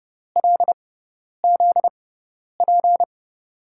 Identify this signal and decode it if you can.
Morse code, heard as a keyed tone, about 30 wpm, spaced as for 10 wpm, 700 Hz. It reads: LZP
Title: USA Call Signs - 6 Characters - 6x Character Spacing - 30wpm